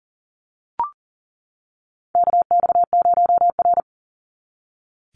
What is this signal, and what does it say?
Morse code, heard as a keyed tone, about 41 wpm, 700 Hz. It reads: KX0R
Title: NA SOTA Chaser Callsigns 40wpm